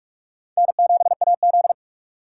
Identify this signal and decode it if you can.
Morse code, heard as a keyed tone, about 42 wpm, 700 Hz. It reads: N7AZ
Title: K1USN SST QSO Practice - 45wpm